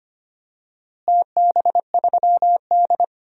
Morse code, heard as a keyed tone, about 25 wpm, 700 Hz. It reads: TB3D
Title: DXCC-100 Call Signs - 4 Characters - Repeated 2x - 25wpm